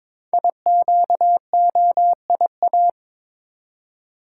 Morse code, heard as a keyed tone, about 22 wpm, 700 Hz. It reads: IQOIA
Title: Code Groups - 5 Letters - 22wpm